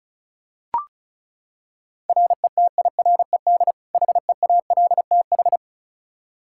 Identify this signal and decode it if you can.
Morse code, heard as a keyed tone, about 36 wpm, 700 Hz. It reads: RETIRED HEALTH
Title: Practice QSO - Exchange 3 - More Info - 35wpm